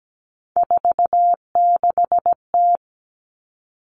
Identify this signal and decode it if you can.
Morse code, heard as a keyed tone, about 17 wpm, 700 Hz. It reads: 46T
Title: All Cut Numbers - 3-Digit 17wpm